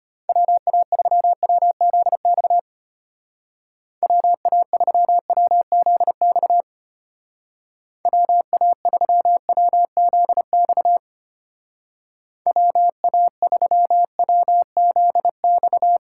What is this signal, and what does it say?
Morse code, heard as a keyed tone, about 32 wpm, 700 Hz. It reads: WA3WZX WA3WZX WA3WZX WA3WZX